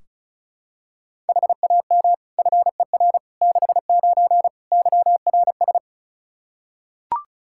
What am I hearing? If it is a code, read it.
Morse code, heard as a keyed tone, about 36 wpm, 700 Hz. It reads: HAM FER 69 YRS
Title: K1USN SST QSO Practice 35wpm